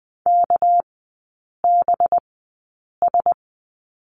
Morse code, heard as a keyed tone, about 20 wpm, spaced as for 8 wpm, 690 Hz. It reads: KBS
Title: 198 Lesson 32e - Letter k - Three Characters 20wpm